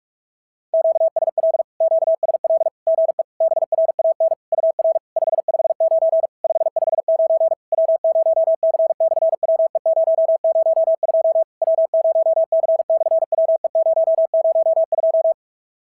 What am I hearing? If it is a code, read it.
Morse code, heard as a keyed tone, about 46 wpm, 690 Hz. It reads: QSL QSL GE BRAN UR 559 559 W0C/WE002 W0C/WE002
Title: SOTA QSO Practice 45wpm